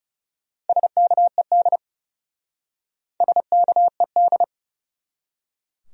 Morse code, heard as a keyed tone, about 37 wpm, 700 Hz. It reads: SKED SKED
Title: Speed-Racing Abbreviations - 30wpm